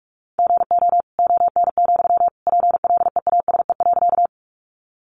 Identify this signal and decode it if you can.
Morse code, heard as a keyed tone, about 44 wpm, 700 Hz. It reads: GO ON, PLEASE.
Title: Sentences from Top 1500 Words 45wpm